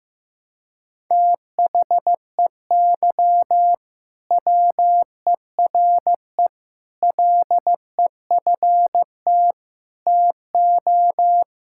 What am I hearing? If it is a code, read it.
Morse code, heard as a keyed tone, about 15 wpm, 700 Hz. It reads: THEY WERE LEFT TO
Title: Sentences from Top 1000 Words - Review Part 5 - 15wpm